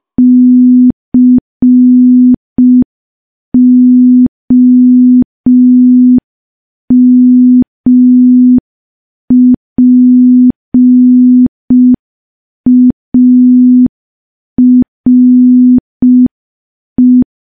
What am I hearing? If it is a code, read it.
Morse code, heard as a keyed tone, about 5 wpm, 250 Hz. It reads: COMPARE
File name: morse / morse-training-05WPM